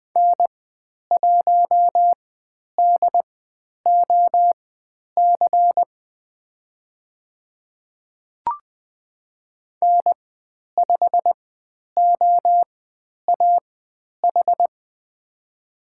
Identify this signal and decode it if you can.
Morse code, heard as a keyed tone, about 20 wpm, spaced as for 10 wpm, 700 Hz. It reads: N1DOC N5OAH
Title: 109 Lesson 18g - Letter m - Call Signs 20wpm